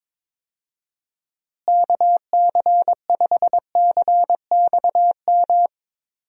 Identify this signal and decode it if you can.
Morse code, heard as a keyed tone, about 22 wpm, 700 Hz. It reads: KC5CXM
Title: USA Call Signs - 6 Characters - Repeated 3x - 22wpm